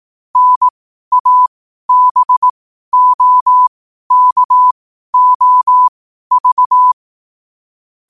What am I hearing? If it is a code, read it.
Morse code, heard as a keyed tone, about 18 wpm, spaced as for 12 wpm, 1000 Hz. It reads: NABOKOV
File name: qotdfiles / 12wpm